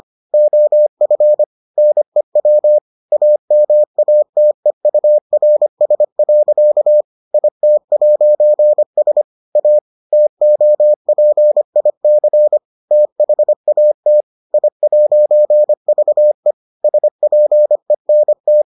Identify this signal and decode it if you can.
Morse code, heard as a keyed tone, about 25 wpm, 600 Hz. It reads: OF NEW AMATEURS. IT'S A TOPIC THAT I'VE SPENT